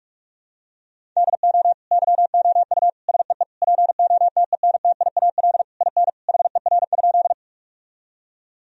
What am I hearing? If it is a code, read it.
Morse code, heard as a keyed tone, about 46 wpm, 700 Hz. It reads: DO YOU SEE POTENTIAL IN HER?
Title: Sentences from Top 1000 Words - Review Part 4 - 45wpm